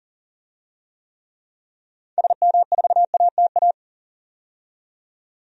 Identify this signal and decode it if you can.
Morse code, heard as a keyed tone, about 40 wpm, 700 Hz. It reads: SM4ATA
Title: DXCC-100 Call Signs - 6 Characters - 40wpm